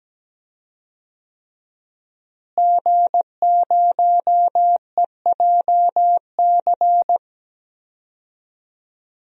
Morse code, heard as a keyed tone, about 17 wpm, 700 Hz. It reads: G0EJC
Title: DXCC-100 Call Signs - 5 Characters - Repeated 3x - 17wpm